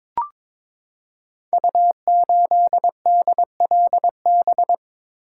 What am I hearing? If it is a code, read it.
Morse code, heard as a keyed tone, about 23 wpm, 700 Hz. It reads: U8DLB
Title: Code Groups - 5 Letter:Numbers - 22wpm